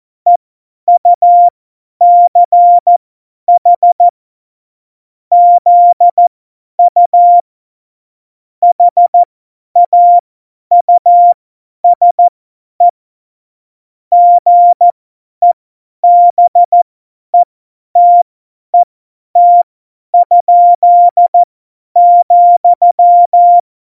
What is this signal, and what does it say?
Morse code, heard as a keyed tone, about 14 wpm, spaced as for 10 wpm, 700 Hz. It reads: EUCH ZU HAUSE GEBETET?,
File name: Morsekurs / Witze / Witz 21. Mai 2019, 50 F70